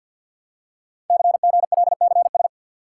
Morse code, heard as a keyed tone, about 46 wpm, 700 Hz. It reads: CGLCS
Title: Code Groups - 5 Letters - 50wpm